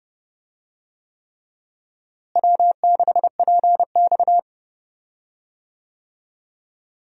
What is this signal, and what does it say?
Morse code, heard as a keyed tone, about 30 wpm, 700 Hz. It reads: W6PX
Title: USA Call Signs - 4 Characters - Repeated 3x - 30wpm